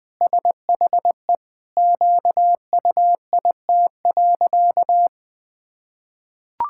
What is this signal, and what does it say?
Morse code, heard as a keyed tone, about 20 wpm, 700 Hz. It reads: SHE QUIT.
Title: Pangram Challenge v2 - 20wpm